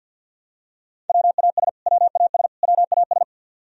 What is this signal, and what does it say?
Morse code, heard as a keyed tone, about 46 wpm, 700 Hz. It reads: WAS WAS WAS
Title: Top 100 Words Repeated 50wpm